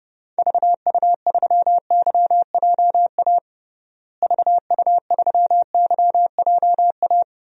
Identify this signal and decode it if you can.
Morse code, heard as a keyed tone, about 30 wpm, 700 Hz. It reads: VU3YJA VU3YJA
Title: DXCC-100 Call Signs - 6 Characters - Repeated 2x - 30wpm